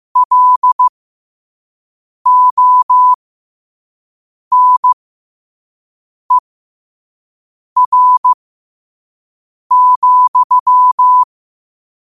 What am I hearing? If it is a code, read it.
Morse code, heard as a keyed tone, about 15 wpm, spaced as for 5 wpm, 1000 Hz. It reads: LONER,
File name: qotdfiles / 5wpm